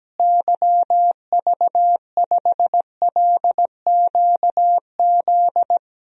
Morse code, heard as a keyed tone, about 17 wpm, 700 Hz. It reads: YV5LQZ
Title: DXCC-100 Call Signs - 6 Characters - Repeated 2x - 17wpm